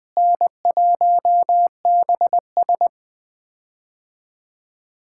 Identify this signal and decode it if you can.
Morse code, heard as a keyed tone, about 20 wpm, 700 Hz. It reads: N1BS
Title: POTA QSO Practice - 20wpm